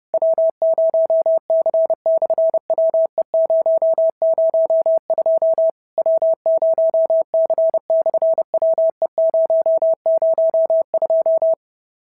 Morse code, heard as a keyed tone, about 30 wpm, 660 Hz. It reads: W0C/WE002 W0C/WE002
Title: SOTA QSO Practice 30wpm